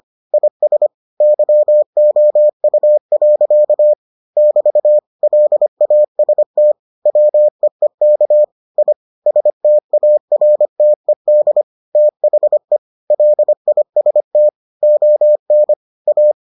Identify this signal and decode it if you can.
Morse code, heard as a keyed tone, about 25 wpm, 600 Hz. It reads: IS YOU. <BT> LAST WEEK I STARTED THE LIST ON A